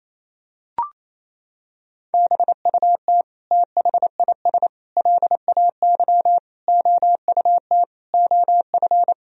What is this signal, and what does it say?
Morse code, heard as a keyed tone, about 28 wpm, 700 Hz. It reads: BUT THIS LAY OUT OF
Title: Sentences from Top 1000 Words - Review Part 2 - 28wpm